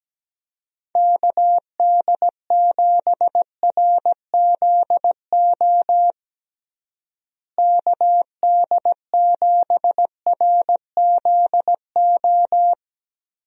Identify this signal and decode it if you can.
Morse code, heard as a keyed tone, about 17 wpm, 700 Hz. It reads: KD7RZO KD7RZO
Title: USA Call Signs - 6 Characters - Repeated 3x - 17wpm